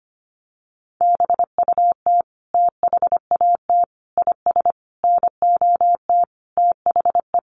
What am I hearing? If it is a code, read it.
Morse code, heard as a keyed tone, about 25 wpm, 700 Hz. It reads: BUT THAT IS NOT THE
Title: Sentences from Top 600 Words 25wpm